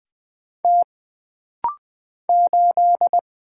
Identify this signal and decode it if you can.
Morse code, heard as a keyed tone, about 20 wpm, 700 Hz. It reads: T 8
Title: Single Letter-Number - Rapid-Fire 20wpm